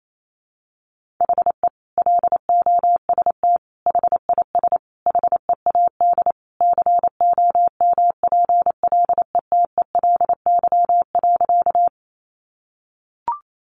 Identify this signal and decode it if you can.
Morse code, heard as a keyed tone, about 28 wpm, 700 Hz. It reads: HE LOST HIS HEAD COMPLETELY.
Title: Sentences from Top 1400 Words 28wpm